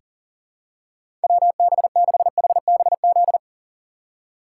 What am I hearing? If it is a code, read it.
Morse code, heard as a keyed tone, about 37 wpm, 700 Hz. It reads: WB6HBZ
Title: USA Call Signs - 6 Characters - 3x Character Spacing - 40wpm